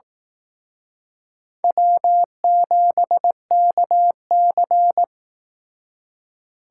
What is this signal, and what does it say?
Morse code, heard as a keyed tone, about 17 wpm, 700 Hz. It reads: W7KC
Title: Speed Racing - USA 1x2 Call Signs 15wpm